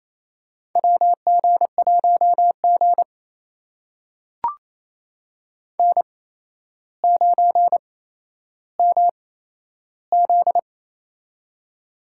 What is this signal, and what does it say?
Morse code, heard as a keyed tone, about 28 wpm, spaced as for 12 wpm, 700 Hz. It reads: WG1G N9MZ
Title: USA Call Signs - 4 Characters - 8x Character Spacing - 28wpm